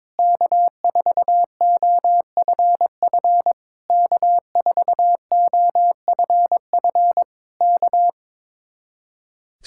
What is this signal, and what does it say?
Morse code, heard as a keyed tone, about 22 wpm, 700 Hz. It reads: K4OFF K4OFF K
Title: Speed-Racing Common QSO Phrases - 22wpm